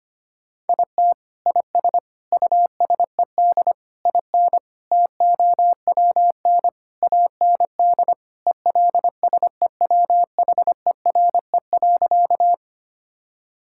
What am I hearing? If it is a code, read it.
Morse code, heard as a keyed tone, about 25 wpm, 700 Hz. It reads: IT IS USED IN TOWN AND ELSEWHERE.
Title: Sentences from Top 1600 Words 25wpm